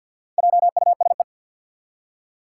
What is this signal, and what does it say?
Morse code, heard as a keyed tone, about 52 wpm, 700 Hz. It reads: JUSE
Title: Four Challenging Letter-Number Combinations 50wpm